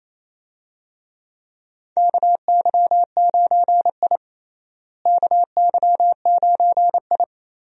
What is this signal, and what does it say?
Morse code, heard as a keyed tone, about 28 wpm, 700 Hz. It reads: KY9I KY9I
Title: USA Call Signs - 4 Characters - Repeated 3x - 28wpm